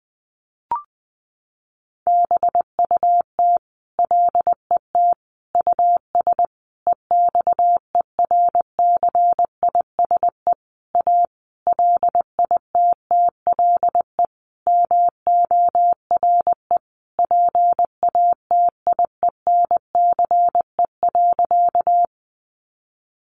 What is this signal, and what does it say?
Morse code, heard as a keyed tone, about 19 wpm, 700 Hz. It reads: BUT LET US EXERCISE A LITTLE MORE PATIENCE.
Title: Sentences from Top 1300 Words 20wpm